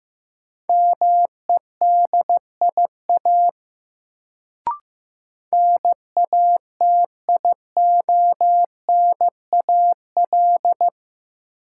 Morse code, heard as a keyed tone, about 15 wpm, 700 Hz. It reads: MEDIA NATIONAL
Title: Sentences from Top 1000 Words - Review Part 4 - New Word Drill - 15wpm